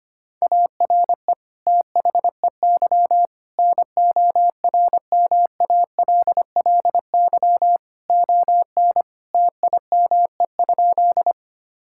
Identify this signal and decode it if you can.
Morse code, heard as a keyed tone, about 25 wpm, 700 Hz. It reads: ARE THEY NORMALLY ON TIME?
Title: Sentences from Top 1000 Words - Review Part 6 - 25wpm